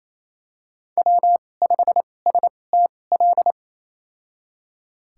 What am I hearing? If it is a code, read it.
Morse code, heard as a keyed tone, about 28 wpm, spaced as for 20 wpm, 700 Hz. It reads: W5STL
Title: USA Call Signs - 5 Characters - 2x Character Spacing - 28wpm